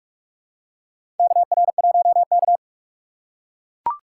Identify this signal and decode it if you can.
Morse code, heard as a keyed tone, about 41 wpm, 700 Hz. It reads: KR1K
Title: USA Call Signs - 4 Characters - 6x Character Spacing - 45wpm